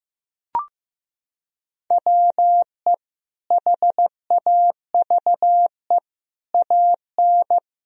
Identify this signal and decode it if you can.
Morse code, heard as a keyed tone, about 15 wpm, 700 Hz. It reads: WE HAVE AN
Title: Sentences from Top 1000 Words - Review Part 3 - 15wpm